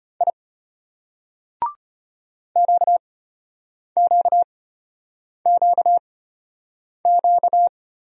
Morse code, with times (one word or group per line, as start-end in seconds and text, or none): 0.2-0.3 I
2.6-7.7 QQQQ